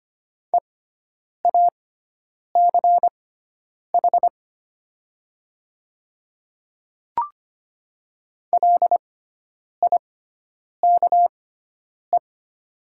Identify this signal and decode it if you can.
Morse code, heard as a keyed tone, about 25 wpm, spaced as for 9 wpm, 700 Hz. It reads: EACH LIKE
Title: 6x Character Spacing - Top 100 Words - 25wpm